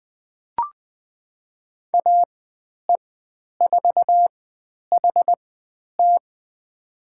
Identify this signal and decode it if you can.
Morse code, heard as a keyed tone, about 20 wpm, spaced as for 10 wpm, 700 Hz. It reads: AE4HT
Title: 102 Lesson 17g - Letter u - Call Signs 20wpm